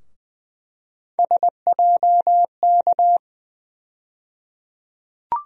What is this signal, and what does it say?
Morse code, heard as a keyed tone, about 21 wpm, 700 Hz. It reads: SJK
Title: USA Call Sign Postfixes - 3 Letters 20wpm